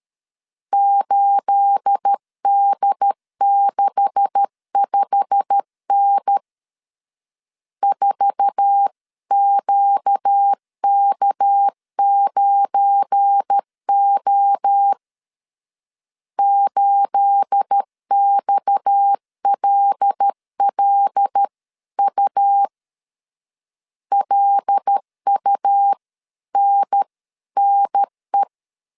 Morse code, heard as a keyed tone, about 13 wpm, 800 Hz. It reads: 8D65N 4QK9O 8XLLU LUNNE
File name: CW / DET / 1 9 1500C S A 12wpm 3800